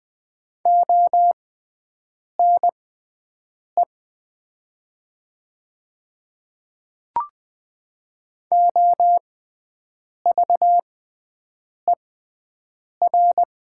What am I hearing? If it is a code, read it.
Morse code, heard as a keyed tone, about 20 wpm, spaced as for 7 wpm, 700 Hz. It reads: ONE OVER
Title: 6x Character Spacing - Top 300 Words - 20wpm